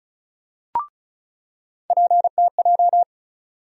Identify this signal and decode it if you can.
Morse code, heard as a keyed tone, about 36 wpm, 700 Hz. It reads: PTJ
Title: USA Call Sign Postfixes - 3 Letters 35wpm